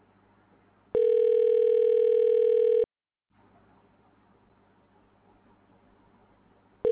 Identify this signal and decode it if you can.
Ringback tone (North America), 440 + 480 Hz.
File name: recordings / REfae25082f729eb41e7c71f22e075a96a